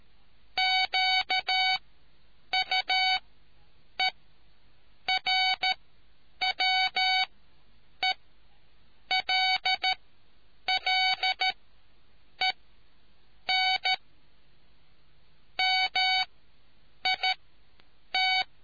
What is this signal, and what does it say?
Morse code, heard as a keyed tone, about 13 wpm, spaced as for 7 wpm, 750 Hz. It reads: QUERWELLEN MIT